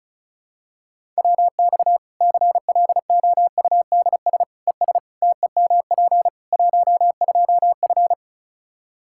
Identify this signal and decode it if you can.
Morse code, heard as a keyed tone, about 34 wpm, 700 Hz. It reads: WX CLOUDS ES TEMP 12F
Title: Practice QSO - Exchange 2 - Info - 35wpm